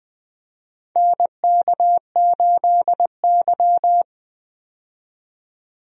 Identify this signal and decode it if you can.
Morse code, heard as a keyed tone, about 20 wpm, 700 Hz. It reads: NK8Y
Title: USA Call Signs - 4 Characters - 20wpm